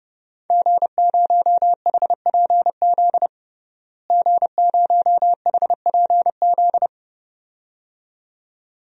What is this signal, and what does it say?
Morse code, heard as a keyed tone, about 30 wpm, 700 Hz. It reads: G0HPZ G0HPZ
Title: DXCC-100 Call Signs - 5 Characters - Repeated 2x - 30wpm